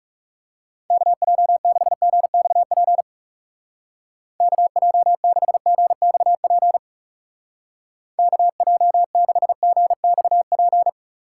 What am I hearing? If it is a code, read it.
Morse code, heard as a keyed tone, about 40 wpm, 700 Hz. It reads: KJ6GXP KJ6GXP KJ6GXP